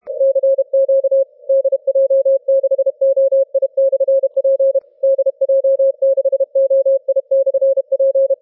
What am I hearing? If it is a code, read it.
Morse code, heard as a keyed tone, about 31 wpm, 540 Hz. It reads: CQ DJ6OI/P DJ6OI/P
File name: wp / dj6oi p eu042 30m cw 02 apr 2012 1806z